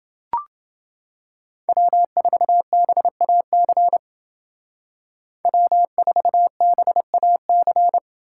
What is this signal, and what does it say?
Morse code, heard as a keyed tone, about 29 wpm, 700 Hz. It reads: W4BAC W4BAC